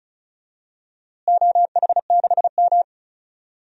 Morse code, heard as a keyed tone, about 34 wpm, 700 Hz. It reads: OH6M
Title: DXCC-100 Call Signs - 4 Characters - Repeated 2x - 35wpm